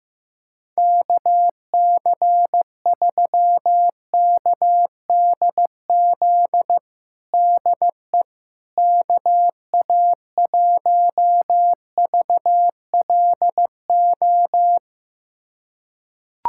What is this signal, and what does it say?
Morse code, heard as a keyed tone, about 15 wpm, 700 Hz. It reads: KC3KDZ DE KA1VLO
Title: USA Callsign DE Callsign - 6 Characters - 15wpm